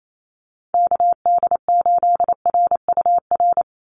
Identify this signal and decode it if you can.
Morse code, heard as a keyed tone, about 28 wpm, 700 Hz. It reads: KD8RUR
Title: USA Call Signs 28wpm